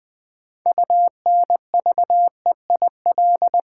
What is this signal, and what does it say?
Morse code, heard as a keyed tone, about 20 wpm, 700 Hz. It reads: UNVEIL
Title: Build A Word - 3 Letters At A Time - 6-Letter Words - 20wpm